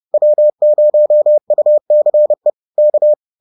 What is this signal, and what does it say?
Morse code, heard as a keyed tone, about 30 wpm, 600 Hz. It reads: W0UCE K
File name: 30wpm-4